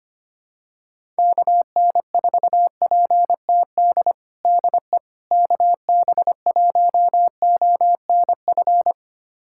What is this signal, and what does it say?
Morse code, heard as a keyed tone, about 25 wpm, 700 Hz. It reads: KN4PTD DE KB1ONF
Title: USA Callsign DE Callsign - 6 Characters - 25wpm